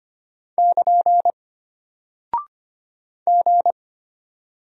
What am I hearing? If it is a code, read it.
Morse code, heard as a keyed tone, about 25 wpm, 700 Hz. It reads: <KN> G
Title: Single Character (Letter:Number:Punctuation:Prosign) Alt 25wpm